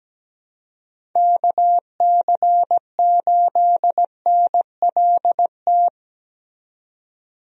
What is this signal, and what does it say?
Morse code, heard as a keyed tone, about 17 wpm, 700 Hz. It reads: KC8NLT
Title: USA Call Signs - 6 Characters - 17wpm